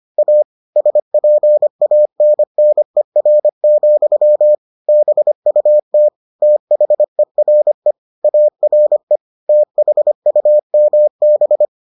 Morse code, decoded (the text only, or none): A SPANNER, BUT THERE ARE THUMB